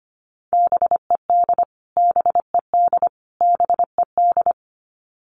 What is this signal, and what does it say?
Morse code, heard as a keyed tone, about 25 wpm, 700 Hz. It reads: BED BED BED
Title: Top 500 Words Repeated 25wpm